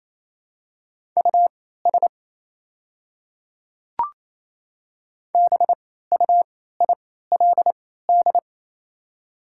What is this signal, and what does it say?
Morse code, heard as a keyed tone, about 27 wpm, spaced as for 16 wpm, 690 Hz. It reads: US BUILD
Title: 3x Character Spacing - Top 300 Words - 28wpm